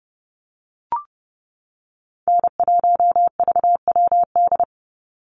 Morse code, heard as a keyed tone, about 30 wpm, 700 Hz. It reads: N1VWD
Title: NA SOTA Chaser Callsigns 30wpm